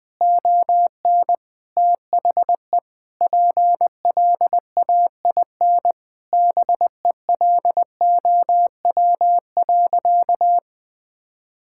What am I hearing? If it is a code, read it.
Morse code, heard as a keyed tone, about 21 wpm, 700 Hz. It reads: ON THE PLAIN BELOW.
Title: Sentences from Top 1000 Words - Review Part 5 - 20wpm